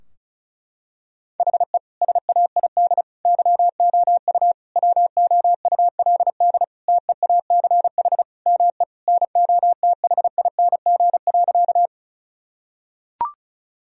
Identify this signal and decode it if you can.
Morse code, heard as a keyed tone, about 34 wpm, 700 Hz. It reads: HE SAID YOU WOULD TEACH ME NOTHING.
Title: Sentences from Top 1000 Words - Review Part 2 - 35wpm